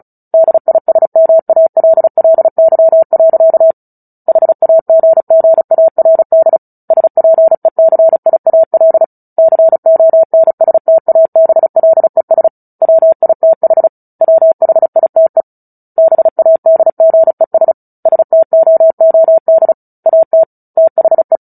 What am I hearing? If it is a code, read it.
Morse code, heard as a keyed tone, about 34 wpm, 650 Hz. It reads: DISMALLY. HAGGARD SPECIAL CONSTABLES WITH WHITE BADGES STOOD AT THE
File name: Book Two / 09-Chapter 9 - 35 WPM